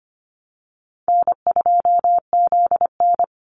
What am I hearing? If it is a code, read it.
Morse code, heard as a keyed tone, about 25 wpm, 700 Hz. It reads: N2ZN
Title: Easy Call Signs Repeated 3x - 25wpm